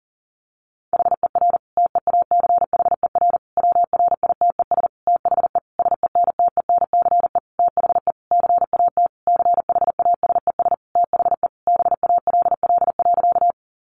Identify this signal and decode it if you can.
Morse code, heard as a keyed tone, about 40 wpm, 700 Hz. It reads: HER TEACHER WRITES THE SENTENCE THE CAT CHASES THE BALL.
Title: Additional Easy Phrases 40wpm